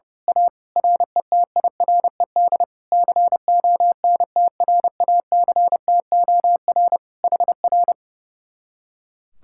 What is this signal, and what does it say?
Morse code, heard as a keyed tone, about 29 wpm, 700 Hz. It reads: A RETIRED CONTRACTOR HR
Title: Speed-Racing Common QSO Phrases - 30wpm